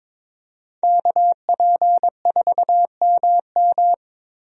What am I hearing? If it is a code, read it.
Morse code, heard as a keyed tone, about 22 wpm, 700 Hz. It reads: KP4MM
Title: USA Call Signs - 5 Characters - Repeated 2x - 22wpm